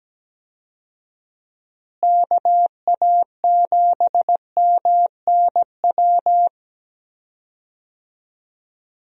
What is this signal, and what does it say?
Morse code, heard as a keyed tone, about 17 wpm, 700 Hz. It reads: KA7MNW
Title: USA Call Signs - 6 Characters - Repeated 3x - 17wpm